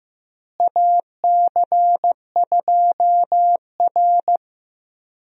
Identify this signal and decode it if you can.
Morse code, heard as a keyed tone, about 15 wpm, 700 Hz. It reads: AC2R
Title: USA Call Signs - 4 Characters - Repeated 2x - 15wpm